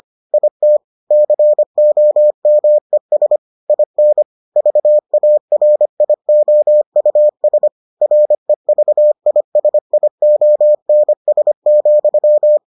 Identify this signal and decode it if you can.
Morse code, heard as a keyed tone, about 25 wpm, 600 Hz. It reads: IT COMES IN VARIOUS REVISIONS,